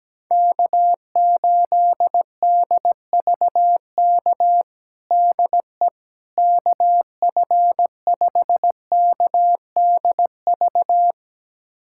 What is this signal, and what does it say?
Morse code, heard as a keyed tone, about 17 wpm, 700 Hz. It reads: K8DVK DE KF5KDV
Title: USA Callsign DE Callsign 17wpm